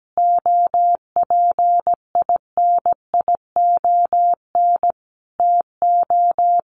Morse code, heard as a keyed tone, about 17 wpm, 700 Hz. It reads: OPINION TO
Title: Sentences from Top 1300 Words 17wpm